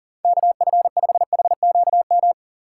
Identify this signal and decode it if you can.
Morse code, heard as a keyed tone, about 52 wpm, 700 Hz. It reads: KF5HQM